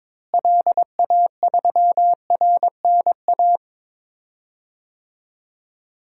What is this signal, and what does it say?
Morse code, heard as a keyed tone, about 21 wpm, 700 Hz. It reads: LA3RNA
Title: DXCC-100 Call Signs - 6 Characters - Repeated 2x - 22wpm